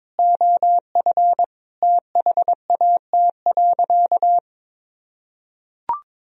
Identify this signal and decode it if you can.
Morse code, heard as a keyed tone, about 22 wpm, 700 Hz. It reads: OF THAT.
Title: Sentences from Top 1000 Words - Review Part 4 - 22wpm